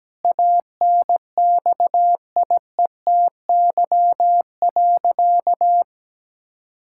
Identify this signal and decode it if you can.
Morse code, heard as a keyed tone, about 17 wpm, 700 Hz. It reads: ANXIETY.
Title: Sentences from Top 1600 Words 17wpm